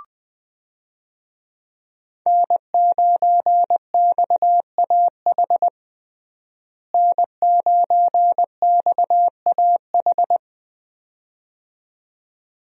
Morse code, heard as a keyed tone, about 20 wpm, 700 Hz. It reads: N9XAH N9XAH
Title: USA Call Signs - 5 Characters - Repeated 2x - 20wpm